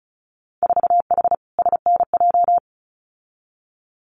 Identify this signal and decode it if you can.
Morse code, heard as a keyed tone, about 35 wpm, 700 Hz. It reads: 4H SNJ
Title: Winter Field Day Exchanges Encore 35wpm